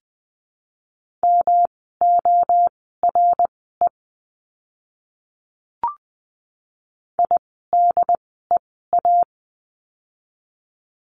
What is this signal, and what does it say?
Morse code, heard as a keyed tone, about 20 wpm, spaced as for 14 wpm, 700 Hz. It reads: MORE IDEA
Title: 108 Lesson 18f - Letter m - Words 20wpm